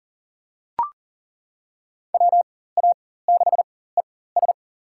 Morse code, heard as a keyed tone, about 40 wpm, spaced as for 19 wpm, 700 Hz. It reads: WA6ES
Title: USA Call Signs - 5 Characters - 4x Character Spacing - 40wpm